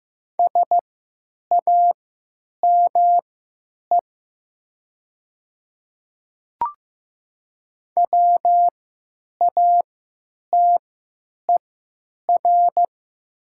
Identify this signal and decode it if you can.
Morse code, heard as a keyed tone, about 15 wpm, spaced as for 9 wpm, 700 Hz. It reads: SAME WATER
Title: 3x Character Spacing - Top 300 Words - 15wpm